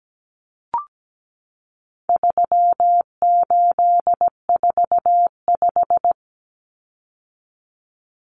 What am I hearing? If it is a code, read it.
Morse code, heard as a keyed tone, about 17 wpm, 700 Hz. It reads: 3845
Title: Common Cut Numbers - 4-Digit 17wpm